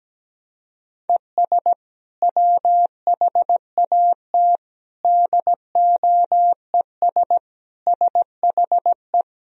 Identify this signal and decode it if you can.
Morse code, heard as a keyed tone, about 17 wpm, 700 Hz. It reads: ES WHAT DOES SHE
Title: Sentences from Top 400 Words and Abbreviations 17wpm